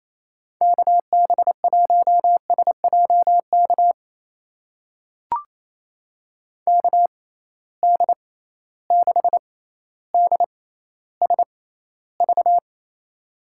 Morse code, heard as a keyed tone, about 28 wpm, spaced as for 14 wpm, 700 Hz. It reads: KB1SJK KD6DSV
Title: USA Call Signs - 6 Characters - 6x Character Spacing - 28wpm